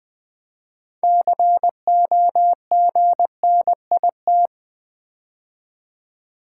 Morse code, heard as a keyed tone, about 20 wpm, 700 Hz. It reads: COGNIT
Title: Build A Word - 3 Letters At A Time - 9-Letter Words - 20wpm